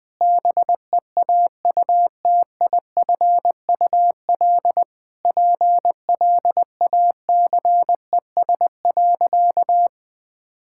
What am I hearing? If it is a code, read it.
Morse code, heard as a keyed tone, about 20 wpm, 700 Hz. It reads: BEAUTIFUL PLACES.
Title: Sentences with 4-Word Phrases - Spoken Beginning - 20wpm